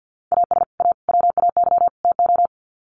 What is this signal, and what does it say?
Morse code, heard as a keyed tone, about 50 wpm, 700 Hz. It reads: AS A WAY TO
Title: Sets of 4 Words 50wpm